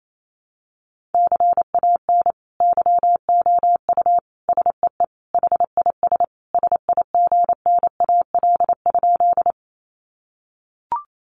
Morse code, heard as a keyed tone, about 28 wpm, 700 Hz. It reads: CAN YOU SEE HIS SIGNAL?
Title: Sentences from Top 1400 Words 28wpm